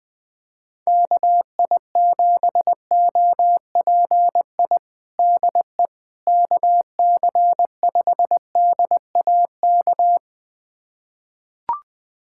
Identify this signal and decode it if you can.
Morse code, heard as a keyed tone, about 21 wpm, 700 Hz. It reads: KI7OPI DE KC5DAK
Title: USA Callsign DE Callsign 20wpm